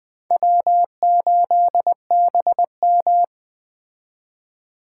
Morse code, heard as a keyed tone, about 20 wpm, 700 Hz. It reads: W8BM
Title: Speed Racing - USA 1x2 Call Signs 17wpm